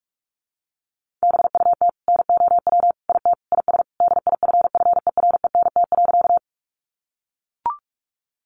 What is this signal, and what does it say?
Morse code, heard as a keyed tone, about 44 wpm, 700 Hz. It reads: BUT NOW IT IS DIFFERENT.
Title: Sentences from Top 600 Words 45wpm